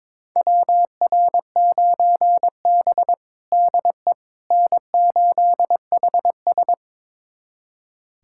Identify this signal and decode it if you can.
Morse code, heard as a keyed tone, about 22 wpm, 700 Hz. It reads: WR9B DE N8HS
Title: USA Callsign DE Callsign - 4 Characters - 22wpm